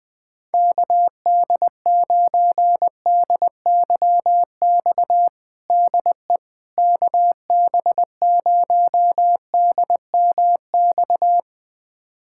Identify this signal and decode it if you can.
Morse code, heard as a keyed tone, about 21 wpm, 700 Hz. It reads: KD9DYX DE KB0DMX
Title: USA Callsign DE Callsign 20wpm